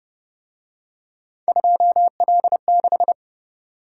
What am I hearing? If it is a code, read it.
Morse code, heard as a keyed tone, about 29 wpm, 700 Hz. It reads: 2L6